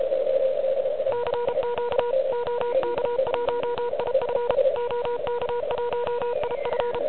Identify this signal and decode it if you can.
Morse code, heard as a keyed tone, about 33 wpm, 960 Hz, with noise. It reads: CQ OK1IF OK1IF
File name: ARCHIV / QCX / QCX Pileup